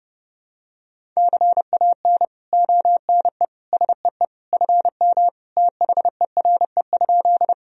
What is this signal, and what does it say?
Morse code, heard as a keyed tone, about 30 wpm, 700 Hz. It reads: CAN ONE SEE FM THERE?
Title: Sentences from Top 100 Words and Abbreviations 30wpm